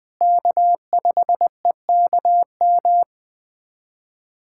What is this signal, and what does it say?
Morse code, heard as a keyed tone, about 20 wpm, 700 Hz. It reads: K5EKM
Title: USA Callsign DE Callsign - 5 Characters - 20wpm